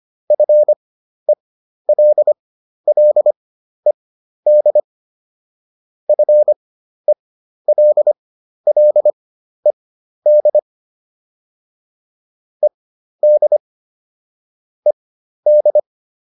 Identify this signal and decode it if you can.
Morse code, heard as a keyed tone, about 25 wpm, spaced as for 12 wpm, 600 Hz. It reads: FELLED FELLED ED ED